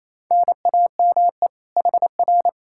Morse code, heard as a keyed tone, about 38 wpm, 700 Hz. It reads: NAME HR